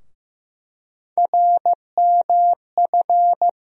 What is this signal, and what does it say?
Morse code, heard as a keyed tone, about 15 wpm, 700 Hz. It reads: RMF
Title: USA Call Sign Postfixes - 3 Letters 15wpm